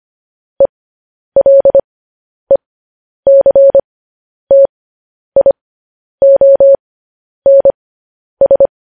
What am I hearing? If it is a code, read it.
Morse code, heard as a keyed tone, about 25 wpm, spaced as for 10 wpm, 550 Hz. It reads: ELECTIONS
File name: podcast / News headlines-1x-25x10